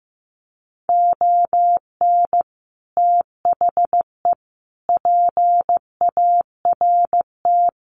Morse code, heard as a keyed tone, about 15 wpm, 700 Hz. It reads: ON THE PART
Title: Sets of 5 Words 15wpm